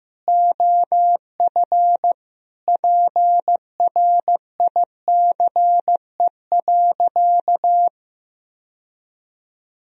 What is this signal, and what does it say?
Morse code, heard as a keyed tone, about 15 wpm, 700 Hz. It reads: OF PRICE.
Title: Sentences from Top 1000 Words - Review Part 4 - 15wpm